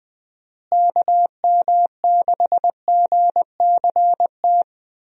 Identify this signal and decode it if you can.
Morse code, heard as a keyed tone, about 20 wpm, 700 Hz. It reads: KM6GCT
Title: USA Call Signs - 6 Characters - Repeated 2x - 20wpm